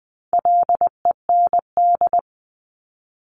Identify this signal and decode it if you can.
Morse code, heard as a keyed tone, about 20 wpm, 700 Hz. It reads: LEND
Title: 260 Lesson 42e - Words 20-20wpm